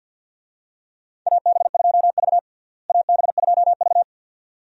Morse code, heard as a keyed tone, about 53 wpm, 700 Hz. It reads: AB2V AB2V
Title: USA Call Signs - 4 Characters - Repeated 2x - 50wpm